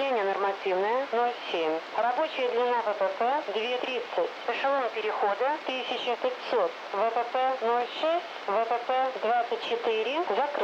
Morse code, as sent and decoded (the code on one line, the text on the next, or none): none